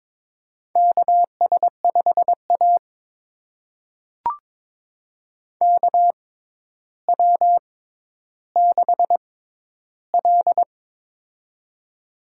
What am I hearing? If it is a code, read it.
Morse code, heard as a keyed tone, about 22 wpm, spaced as for 11 wpm, 700 Hz. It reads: KS5A KW6L
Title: USA Call Signs - 4 Characters - 6x Character Spacing - 22wpm